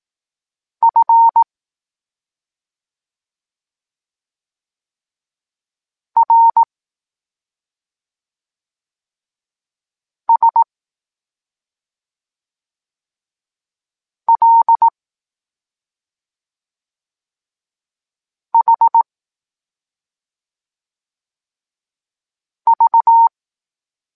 Morse code, heard as a keyed tone, about 18 wpm, 920 Hz. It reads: F R S L H V
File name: Tekst-26 1